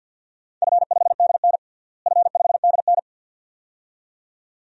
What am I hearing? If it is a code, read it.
Morse code, heard as a keyed tone, about 51 wpm, 700 Hz. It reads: F5DN F5DN
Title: DXCC-100 Call Signs - 4 Characters - Repeated 2x - 50wpm